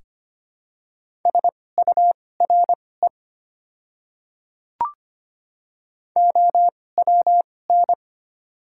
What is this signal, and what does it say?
Morse code, heard as a keyed tone, about 25 wpm, spaced as for 18 wpm, 700 Hz. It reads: SURE OWN
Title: 2x Character Spacing - Top 300 Words - 25wpm